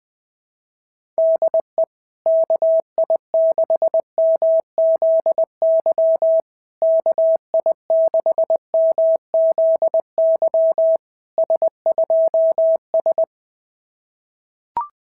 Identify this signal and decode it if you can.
Morse code, heard as a keyed tone, about 19 wpm, 650 Hz. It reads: DE KI6MZY KI6MZY S2S
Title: SOTA QSO Practice 20wpm